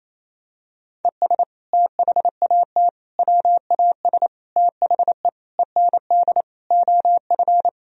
Morse code, heard as a keyed tone, about 28 wpm, 700 Hz. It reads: ES THAT WAS THE END OF
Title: Sentences from Top 200 Words and Abbreviations 28wpm